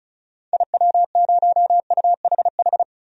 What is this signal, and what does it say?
Morse code, heard as a keyed tone, about 36 wpm, 700 Hz. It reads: IW0UHH
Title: DXCC-100 Call Signs - 6 Characters - 35wpm